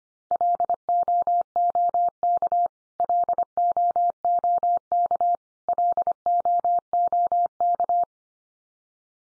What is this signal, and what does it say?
Morse code, heard as a keyed tone, about 25 wpm, 700 Hz. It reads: LOOK LOOK LOOK
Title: Top 100 Words 25wpm x3